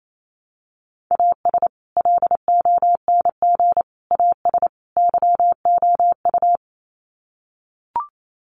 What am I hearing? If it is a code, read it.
Morse code, heard as a keyed tone, about 28 wpm, 700 Hz. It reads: AS LONG AS YOU
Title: Sets of 4 Words 28wpm